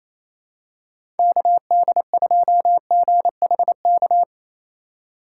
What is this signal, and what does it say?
Morse code, heard as a keyed tone, about 27 wpm, 700 Hz. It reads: KD2GHK
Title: USA Call Signs - 4x Character Spacing - 28wpm